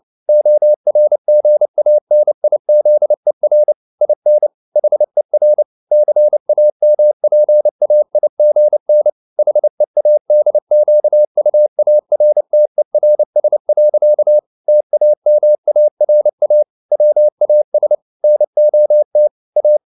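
Morse code, heard as a keyed tone, about 29 wpm, 600 Hz. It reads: ORGANIZER IN HER CAMPAIGN HEADQUARTERS. TAMARA WAS NOT A
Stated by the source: Morse code at 29 wpm, sending Ken Follett's novel Never in English, from CW med Gnister.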